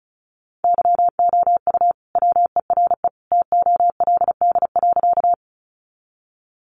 Morse code, heard as a keyed tone, about 34 wpm, 700 Hz. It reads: YOU WERE TOLD.
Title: Sentences from Top 400 Words 35wpm